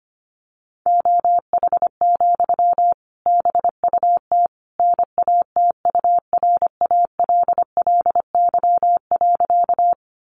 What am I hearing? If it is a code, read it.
Morse code, heard as a keyed tone, about 25 wpm, 700 Hz. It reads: OH, BUT NATURALLY.
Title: Sentences from Top 1300 Words 25wpm